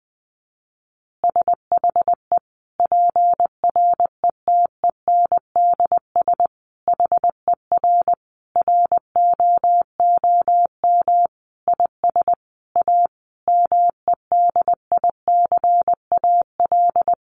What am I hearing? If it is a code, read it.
Morse code, heard as a keyed tone, about 20 wpm, 700 Hz. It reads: SHE PRETENDS HER ROOM IS A MEDICAL
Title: Additional Easy Phrases 20wpm